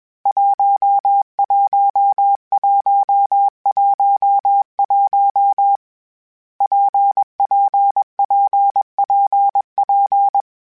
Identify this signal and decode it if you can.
Morse code, heard as a keyed tone, about 21 wpm, 800 Hz. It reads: 11111 PPPPP
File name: Difficult Combos / Difficult Combos-mp3s / WPJ WPJ1 @20